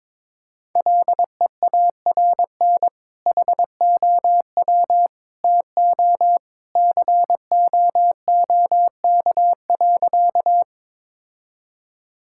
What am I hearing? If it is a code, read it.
Morse code, heard as a keyed tone, about 22 wpm, 700 Hz. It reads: LEARN HOW TO COOK.
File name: practice / 22wpm / Sentences with 2-Word Phrases - 22wpm